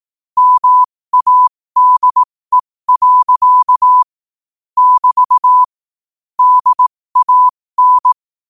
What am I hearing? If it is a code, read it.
Morse code, heard as a keyed tone, about 18 wpm, spaced as for 15 wpm, 1000 Hz. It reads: MADE. <BT> DAN
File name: qotdfiles / 15wpm